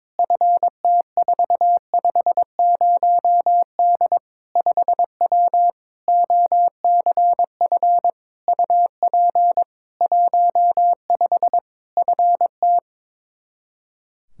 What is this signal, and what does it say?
Morse code, heard as a keyed tone, about 22 wpm, 700 Hz. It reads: FT450D 5W OCF UP 15 FT
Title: K1USN SST QSO Practice 22wpm